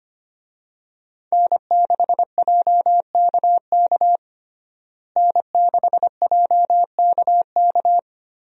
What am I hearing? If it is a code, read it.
Morse code, heard as a keyed tone, about 25 wpm, 700 Hz. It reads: N6JKK N6JKK
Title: USA Call Signs - 5 Characters - Repeated 3x - 25wpm